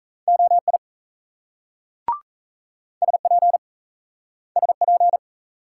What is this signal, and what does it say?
Morse code, heard as a keyed tone, about 36 wpm, 700 Hz. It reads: OI SP SP